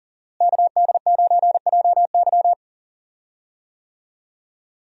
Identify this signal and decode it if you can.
Morse code, heard as a keyed tone, about 40 wpm, 700 Hz. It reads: KD9JY
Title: USA Call Signs - 5 Characters - Repeated 2x - 40wpm